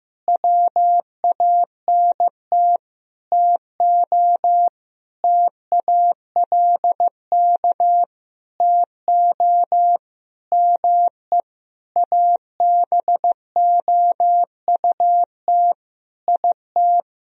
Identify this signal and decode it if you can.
Morse code, heard as a keyed tone, about 15 wpm, 700 Hz. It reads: WANT TO TALK TO ME ABOUT IT
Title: Sentences from Top 1000 Words - Review Part 4 - 15wpm